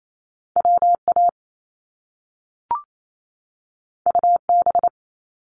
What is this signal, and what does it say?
Morse code, heard as a keyed tone, about 28 wpm, 700 Hz. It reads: WA UB
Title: Top Two Letter Combinations in Words 28wpm